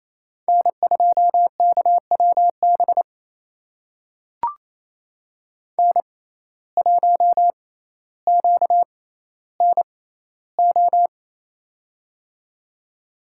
Morse code, heard as a keyed tone, about 28 wpm, spaced as for 14 wpm, 700 Hz. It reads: N2KWB N1QNO
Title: USA Call Signs - 6x Character Spacing - 28wpm